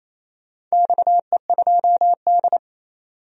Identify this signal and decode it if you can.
Morse code, heard as a keyed tone, about 27 wpm, 700 Hz. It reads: XE2D
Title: DXCC-100 Call Signs - 4 Characters - 28wpm